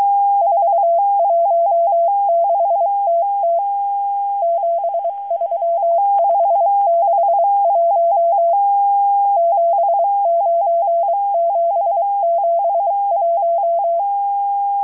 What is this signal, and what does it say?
Morse code, heard as a keyed tone, about 23 wpm, 680 Hz. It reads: 416TT 73561 79771
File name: HM02- -0437utc-7351khz